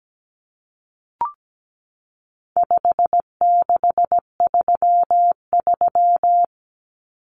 Morse code, heard as a keyed tone, about 17 wpm, 700 Hz. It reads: 5633
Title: Numbers - 4-Digit 17wpm